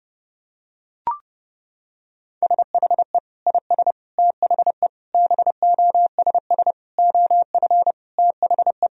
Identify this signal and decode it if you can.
Morse code, heard as a keyed tone, about 32 wpm, 700 Hz. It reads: SHE IS THE BOSS OF THE
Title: Sentences from Top 1000 Words - Review Part 4 - 30wpm